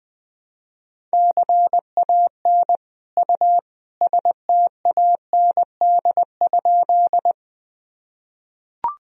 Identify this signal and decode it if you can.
Morse code, heard as a keyed tone, about 20 wpm, 700 Hz. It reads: CAN U STAND?
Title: Sentences from Top 300 Words and Abbreviations 20wpm